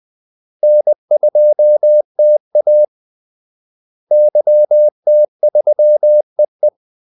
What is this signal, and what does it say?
Morse code, heard as a keyed tone, about 20 wpm, 600 Hz. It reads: N2TA YT3EE
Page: Kaldesignaler 20 wpm